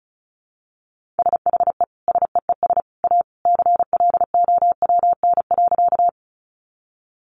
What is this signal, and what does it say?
Morse code, heard as a keyed tone, about 34 wpm, 700 Hz. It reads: SHE SEES A CLOWN.
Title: Easy Phrases 35wpm